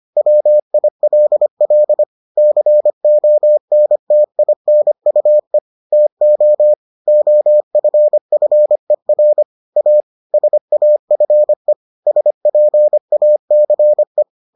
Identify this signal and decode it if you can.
Morse code, heard as a keyed tone, about 25 wpm, 600 Hz. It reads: WILL CONTINUE TO OFFER A SAFE SPACE